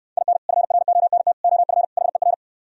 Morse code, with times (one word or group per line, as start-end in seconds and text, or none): 0.2-0.4 IN
0.5-1.3 FRONT
1.4-1.8 OF
2.0-2.3 HER